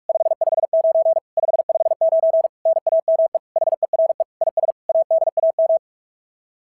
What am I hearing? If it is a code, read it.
Morse code, heard as a keyed tone, about 44 wpm, 650 Hz. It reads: 559 559 NAME HERE IS ADAM
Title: SOTA QSO Practice 45wpm